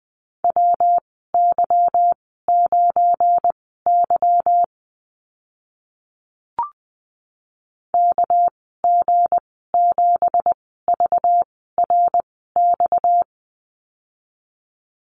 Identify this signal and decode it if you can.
Morse code, heard as a keyed tone, about 20 wpm, spaced as for 14 wpm, 700 Hz. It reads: WY9Y KG7VRX
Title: 238 Lesson 38g - Letter q - Call Signs 20wpm